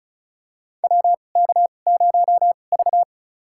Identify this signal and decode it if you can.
Morse code, heard as a keyed tone, about 36 wpm, spaced as for 25 wpm, 690 Hz. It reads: WK0V
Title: USA Call Signs - 4 Characters - 2x Character Spacing - 35wpm